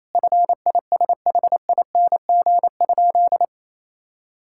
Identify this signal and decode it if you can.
Morse code, heard as a keyed tone, about 28 wpm, 700 Hz. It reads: FISHING?
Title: Sentences from Top 1000 Words - Review Part 5 - 28wpm